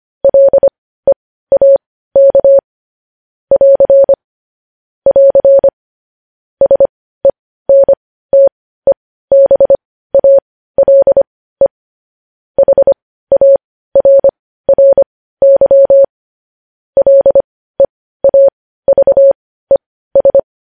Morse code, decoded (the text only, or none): LEAK <AR> <AR> SENTEBALE HARRY LEAVES